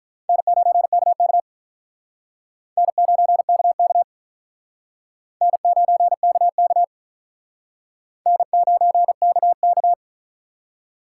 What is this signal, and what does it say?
Morse code, heard as a keyed tone, about 48 wpm, 700 Hz. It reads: N9KK N9KK N9KK N9KK